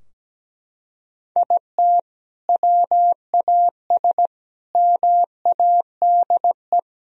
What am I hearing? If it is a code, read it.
Morse code, heard as a keyed tone, about 17 wpm, 700 Hz. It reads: IT WAS MADE
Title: Sentences from Top 1000 Words - Review Part 4 - 17wpm